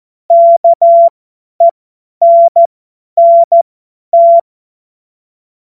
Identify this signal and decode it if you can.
Morse code, heard as a keyed tone, about 14 wpm, spaced as for 10 wpm, 680 Hz. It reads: KENNT
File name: Morsekurs / Witze / Witz 30. November 2021, 50 F70